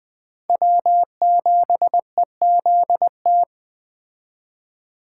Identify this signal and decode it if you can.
Morse code, heard as a keyed tone, about 21 wpm, 700 Hz. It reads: W7EZT
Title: Speed Racing - USA Call Signs 17wpm